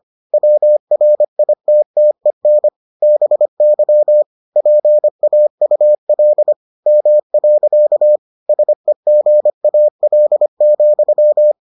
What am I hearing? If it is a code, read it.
Morse code, heard as a keyed tone, about 25 wpm, 600 Hz. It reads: WRITTEN BY PAUL M. SEGAL,